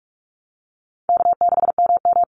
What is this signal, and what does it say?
Morse code, heard as a keyed tone, about 46 wpm, 700 Hz. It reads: K6MM
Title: Call Signs - 45wpm